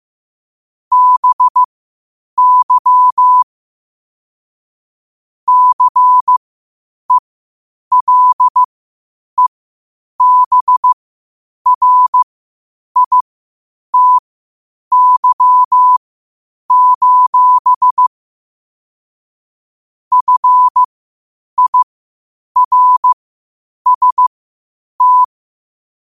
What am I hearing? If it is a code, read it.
Morse code, heard as a keyed tone, about 15 wpm, spaced as for 8 wpm, 1000 Hz. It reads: BY CELEBRITY: FIRST
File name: qotdfiles / 8wpm